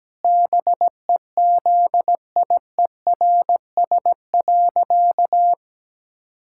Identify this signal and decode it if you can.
Morse code, heard as a keyed tone, about 17 wpm, 700 Hz. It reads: BEZIERS.
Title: Pangram Challenge v2 - 17wpm